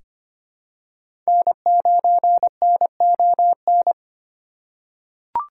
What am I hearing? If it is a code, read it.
Morse code, heard as a keyed tone, about 25 wpm, 700 Hz. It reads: N9NON
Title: USA Call Signs - 5 Characters - 2x Character Spacing - 25wpm